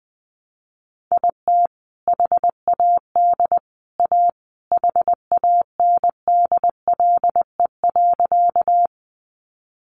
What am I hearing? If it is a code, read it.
Morse code, heard as a keyed tone, about 20 wpm, 700 Hz. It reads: IT HAD A HANDLE.
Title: Sentences from Top 1100 Words 20wpm